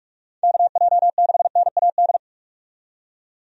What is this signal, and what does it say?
Morse code, heard as a keyed tone, about 42 wpm, 700 Hz. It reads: KJ6NAD
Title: USA Call Signs - 6 Characters - 45wpm